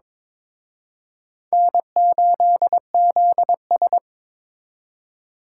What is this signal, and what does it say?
Morse code, heard as a keyed tone, about 26 wpm, 700 Hz. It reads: N8ZS
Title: Speed Racing - USA Call Signs 22wpm